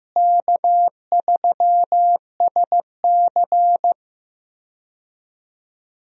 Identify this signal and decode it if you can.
Morse code, heard as a keyed tone, about 15 wpm, 700 Hz. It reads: K3SC
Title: USA Call Signs - 4 Characters - Repeated 2x - 15wpm